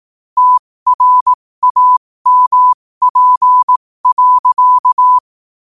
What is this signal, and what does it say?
Morse code, heard as a keyed tone, about 18 wpm, spaced as for 15 wpm, 1000 Hz. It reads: TRAMP.
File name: qotdfiles / 15wpm